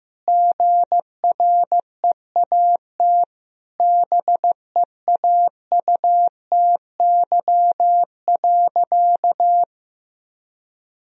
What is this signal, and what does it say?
Morse code, heard as a keyed tone, about 16 wpm, 700 Hz. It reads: GREAT BEAUTY.
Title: Sentences from Top 1000 Words - Review Part 2 - 15wpm